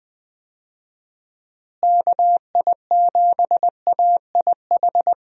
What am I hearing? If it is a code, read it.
Morse code, heard as a keyed tone, about 20 wpm, 700 Hz. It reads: KI7AIH
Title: USA Call Signs - 6 Characters - Repeated 3x - 20wpm